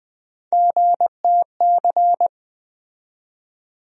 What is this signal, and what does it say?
Morse code, heard as a keyed tone, about 20 wpm, 700 Hz. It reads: GTC
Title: USA Call Sign Postfixes - 3 Letters 20wpm